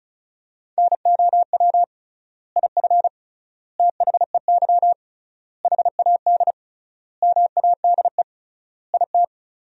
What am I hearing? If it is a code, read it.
Morse code, heard as a keyed tone, about 34 wpm, 700 Hz. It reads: NOW IF THEY HAD MADE IT
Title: 3x Word-Spacing - Sentences from Top 500 Words - 35wpm